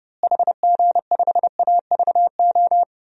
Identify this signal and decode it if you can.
Morse code, heard as a keyed tone, about 32 wpm, 700 Hz. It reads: HG5AVO
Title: DXCC-100 Call Signs - 6 Characters - Repeated 2x - 30wpm